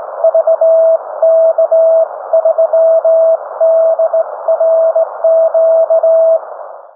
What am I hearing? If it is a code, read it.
Morse code, heard as a keyed tone, about 18 wpm, 660 Hz. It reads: VK3DRQ